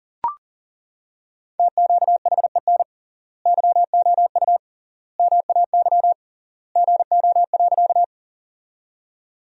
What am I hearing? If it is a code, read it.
Morse code, heard as a keyed tone, about 40 wpm, 700 Hz. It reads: TQHEN YOU MAY GO.
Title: 3x Word-Spacing - Sentences from Top 100 Words - 40wpm